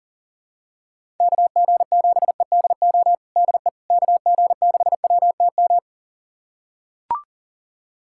Morse code, heard as a keyed tone, about 40 wpm, 700 Hz. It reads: KG7EDO DE KG6WTM
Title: USA Callsign DE Callsign 40wpm